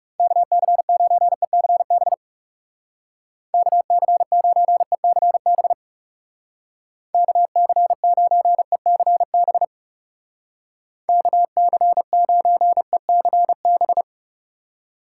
Morse code, heard as a keyed tone, about 40 wpm, 700 Hz. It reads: KC9ECB KC9ECB KC9ECB KC9ECB